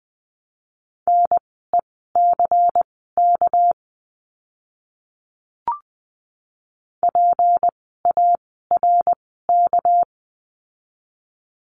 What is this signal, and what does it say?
Morse code, heard as a keyed tone, about 20 wpm, spaced as for 14 wpm, 700 Hz. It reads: NECK PARK
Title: 199 Lesson 32f - Letter k - Words 20wpm